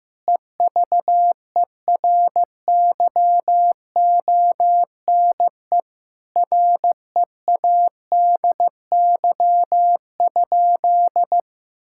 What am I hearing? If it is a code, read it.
Morse code, heard as a keyed tone, about 15 wpm, 700 Hz. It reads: EVERYONE READY?
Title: Sentences from Top 1000 Words - Review Part 5 - 15wpm